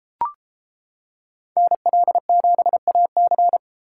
Morse code, heard as a keyed tone, about 33 wpm, 700 Hz. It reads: NL7AC